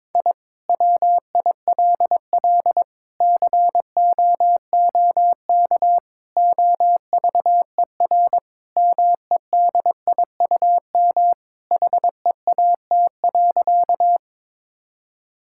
Morse code, heard as a keyed tone, about 22 wpm, 700 Hz. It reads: I WILL COOK OVER MEDIUM HEAT.
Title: Sentences from Top 1000 Words - Review Part 6 - 22wpm